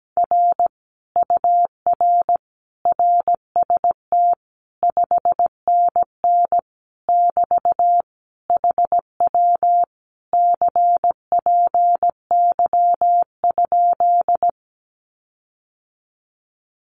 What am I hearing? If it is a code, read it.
Morse code, heard as a keyed tone, about 17 wpm, 700 Hz. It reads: R UR RST 5NN <BT> HW CPY?
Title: Common QSO Phrases Encore 17wpm